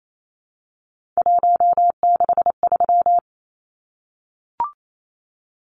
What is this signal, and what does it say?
Morse code, heard as a keyed tone, about 28 wpm, 700 Hz. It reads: 163
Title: Numbers - 3-Digit 28wpm